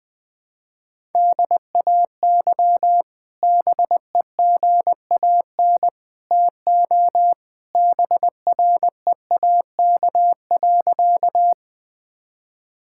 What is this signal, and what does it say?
Morse code, heard as a keyed tone, about 20 wpm, 700 Hz. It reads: DAY BEGAN TO BREAK.
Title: Sentences from Top 1000 Words - Review Part 6 - 20wpm